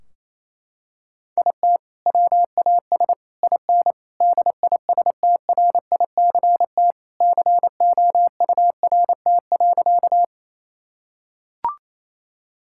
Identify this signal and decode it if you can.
Morse code, heard as a keyed tone, about 28 wpm, 700 Hz. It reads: IT WAS IN DISTRICT COURT.
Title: Sentences from Top 1600 Words 28wpm